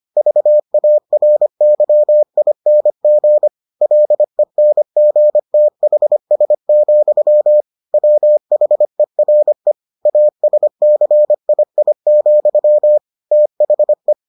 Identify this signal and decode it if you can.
Morse code, heard as a keyed tone, about 25 wpm, 600 Hz. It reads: VARYING LENGTHS, WHERE ASCII, THE